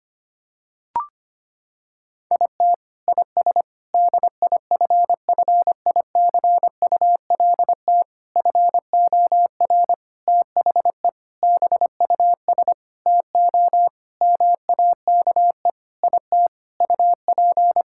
Morse code, heard as a keyed tone, about 25 wpm, 700 Hz. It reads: IT IS DIFFICULT FOR THE BUS TO MAKE IT UP
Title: Sentences from Top 1000 Words - Review Part 6 - 25wpm